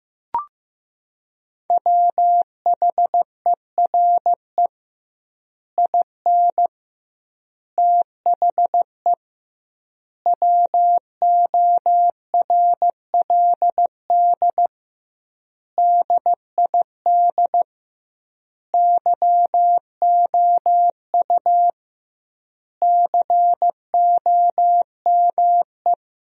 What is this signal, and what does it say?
Morse code, heard as a keyed tone, about 15 wpm, 700 Hz. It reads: WHERE IN THE WORLD DID YOU COME
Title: 2x Word-Spacing - Sentences from Top 200 Words - 15wpm